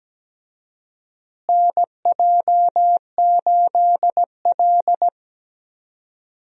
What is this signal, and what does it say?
Morse code, heard as a keyed tone, about 17 wpm, 700 Hz. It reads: NJ8L
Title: USA Call Signs - 4 Characters - Repeated 3x - 17wpm